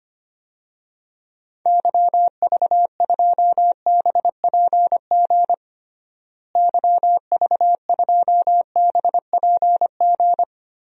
Morse code, heard as a keyed tone, about 25 wpm, 700 Hz. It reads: YV2BPG YV2BPG
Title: DXCC-100 Call Signs - 6 Characters - Repeated 2x - 25wpm